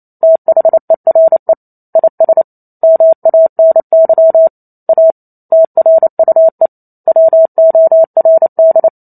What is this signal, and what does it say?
Morse code, heard as a keyed tone, about 28 wpm, 650 Hz. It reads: THERE IS MANY A TRUE WORD